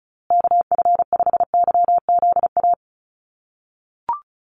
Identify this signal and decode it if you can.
Morse code, heard as a keyed tone, about 34 wpm, 700 Hz. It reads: KF5YZA
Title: USA Call Signs Repeated 3x 35wpm